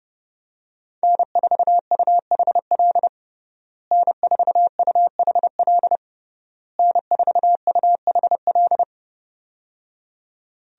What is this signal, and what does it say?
Morse code, heard as a keyed tone, about 29 wpm, 700 Hz. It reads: N4UHL N4UHL N4UHL
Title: USA Call Signs - 5 Characters - Repeated 3x - 30wpm